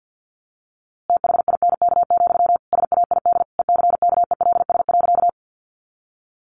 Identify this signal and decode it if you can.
Morse code, heard as a keyed tone, about 50 wpm, 700 Hz. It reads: THINK, SAID ECKELS.
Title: A Sound Of Thunder 50wpm